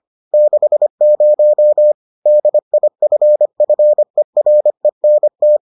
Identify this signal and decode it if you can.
Morse code, heard as a keyed tone, about 25 wpm, 600 Hz. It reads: 60 DIFFERENT